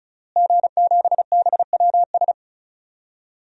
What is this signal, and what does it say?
Morse code, heard as a keyed tone, about 34 wpm, 700 Hz. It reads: G7BWS
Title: DXCC-100 Call Signs - 5 Characters - Repeated 3x - 35wpm